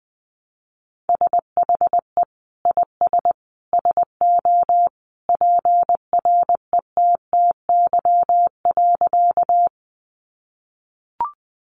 Morse code, heard as a keyed tone, about 20 wpm, 700 Hz. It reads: SHE IS SO PRETTY.
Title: Sentences from Top 1000 Words 20wpm